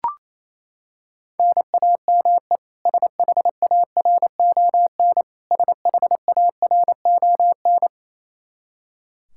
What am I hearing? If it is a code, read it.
Morse code, heard as a keyed tone, about 28 wpm, 700 Hz. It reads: NAME SHARON SHARON
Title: Practice QSO - Exchange 1 - Beginning - 28wpm